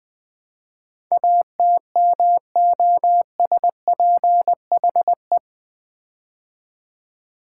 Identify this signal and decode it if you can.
Morse code, heard as a keyed tone, about 20 wpm, 700 Hz. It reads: ATMOSPHE
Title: Build A Word - 4 Letters At A Time - 10-Letter Words - 20wpm